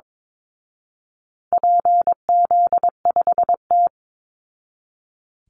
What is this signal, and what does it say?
Morse code, heard as a keyed tone, about 22 wpm, 700 Hz. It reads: PZ5T
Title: Call Signs Repeated 2x - 22wpm